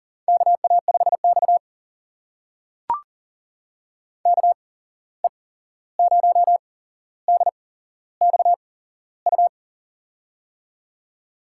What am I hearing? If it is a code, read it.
Morse code, heard as a keyed tone, about 40 wpm, spaced as for 15 wpm, 700 Hz. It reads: KA5X KE0DXU
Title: USA Call Signs - 8x Character Spacing - 40wpm